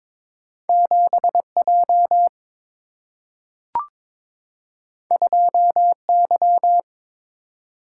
Morse code, heard as a keyed tone, about 22 wpm, 700 Hz. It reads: 7 J 2 Y
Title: Single-Letter Single-Number 22wpm